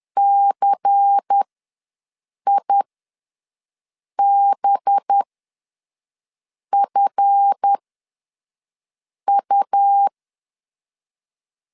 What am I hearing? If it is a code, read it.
Morse code, heard as a keyed tone, about 10 wpm, spaced as for 5 wpm, 800 Hz. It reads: CIBFU
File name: LEZIONI / 8 YJ45 / 8 YJ4 ws 10wpm 4020